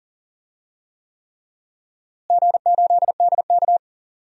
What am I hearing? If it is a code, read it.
Morse code, heard as a keyed tone, about 40 wpm, 700 Hz. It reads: G8DK
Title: DXCC-100 Call Signs - 4 Characters - Repeated 3x - 40wpm